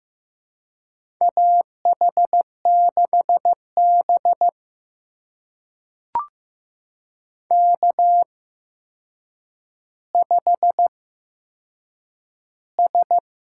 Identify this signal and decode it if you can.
Morse code, heard as a keyed tone, about 15 wpm, spaced as for 7 wpm, 700 Hz. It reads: AH6B K5S
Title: USA Call Signs - 4 Characters - 8x Character Spacing - 15wpm